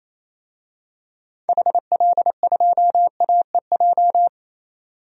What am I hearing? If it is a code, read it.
Morse code, heard as a keyed tone, about 28 wpm, 700 Hz. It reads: HL2AEJ
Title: DXCC-100 Call Signs - 6 Characters - 28wpm